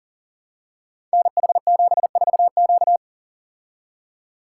Morse code, heard as a keyed tone, about 40 wpm, 700 Hz. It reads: NH74Q
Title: Code Groups - 5 Letter:Numbers - 40wpm